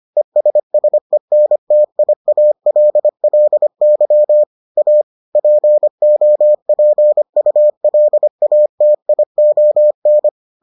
Morse code, heard as a keyed tone, about 25 wpm, 600 Hz. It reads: ESSENTIALLY A POPULATION